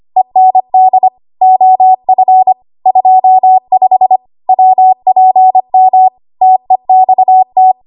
Morse code, heard as a keyed tone, about 25 wpm, 750 Hz. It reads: END OF 25 WPM TEXT
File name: Archive / 25 WPM / 25WPM